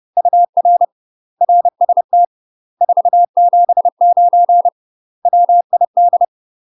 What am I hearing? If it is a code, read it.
Morse code, heard as a keyed tone, about 30 wpm, 700 Hz. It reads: UR RST 479 WID
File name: UK MASTER QSOs TRAINING / 30WPM / UK MASTER QSOs 0002